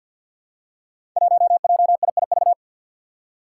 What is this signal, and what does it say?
Morse code, heard as a keyed tone, about 46 wpm, 700 Hz. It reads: 1JIIV
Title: Five Challenging Letter-Number Combinations 50wpm